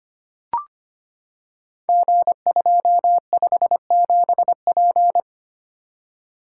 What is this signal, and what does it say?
Morse code, heard as a keyed tone, about 25 wpm, 700 Hz. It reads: G257P
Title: Code Groups - 5 Letter:Numbers - 25wpm